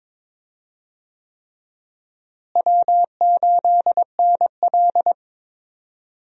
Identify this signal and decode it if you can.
Morse code, heard as a keyed tone, about 22 wpm, 700 Hz. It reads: W8NL
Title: USA Call Signs - 4 Characters - Repeated 3x - 22wpm